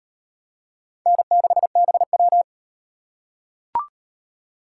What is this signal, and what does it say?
Morse code, heard as a keyed tone, about 38 wpm, 700 Hz. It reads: N6BW